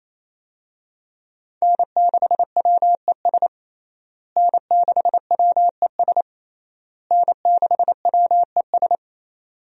Morse code, heard as a keyed tone, about 28 wpm, 700 Hz. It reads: N6WES N6WES N6WES
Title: USA Call Signs - 5 Characters - Repeated 3x - 28wpm